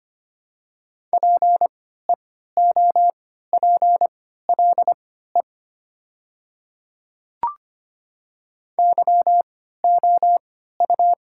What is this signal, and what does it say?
Morse code, heard as a keyed tone, about 25 wpm, spaced as for 14 wpm, 700 Hz. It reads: PEOPLE YOU
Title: 3x Character Spacing - Top 200 Words - 25wpm